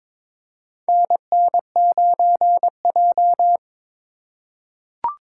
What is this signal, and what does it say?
Morse code, heard as a keyed tone, about 22 wpm, 700 Hz. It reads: NN9J
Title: USA Call Signs - 4 Characters - 4x Character Spacing - 22wpm